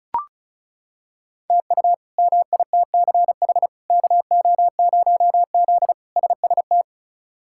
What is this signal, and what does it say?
Morse code, heard as a keyed tone, about 36 wpm, 700 Hz. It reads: TU MITCH KO0Z SST
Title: K1USN SST QSO Practice - 35wpm